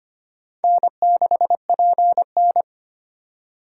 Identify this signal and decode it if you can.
Morse code, heard as a keyed tone, about 25 wpm, 700 Hz. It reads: N6PN
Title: USA Call Signs - 4 Characters - 3x Character Spacing - 25wpm